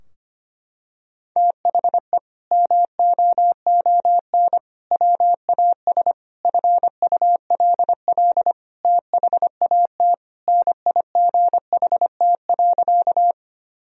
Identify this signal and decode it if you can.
Morse code, heard as a keyed tone, about 25 wpm, 700 Hz. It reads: THE MOON WAS FULL THAT NIGHT.
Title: Sentences from Top 1000 Words - Review Part 3 - 25wpm